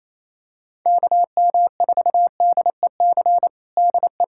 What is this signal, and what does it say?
Morse code, heard as a keyed tone, about 28 wpm, 700 Hz. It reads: KM4DEC DE
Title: USA Callsign DE Callsign 28wpm